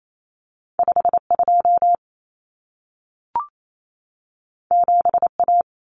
Morse code, heard as a keyed tone, about 28 wpm, 700 Hz. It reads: 52 7A
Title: Common Cut Numbers - 2-Digit 28wpm